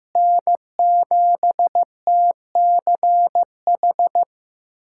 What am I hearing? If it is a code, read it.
Morse code, heard as a keyed tone, about 15 wpm, 700 Hz. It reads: N7TCH
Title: USA Callsign DE Callsign - 5 Characters - 15wpm